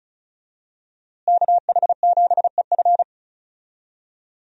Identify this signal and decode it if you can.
Morse code, heard as a keyed tone, about 36 wpm, 700 Hz. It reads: KH7EF
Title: USA Call Signs - 5 Characters - Repeated 2x - 35wpm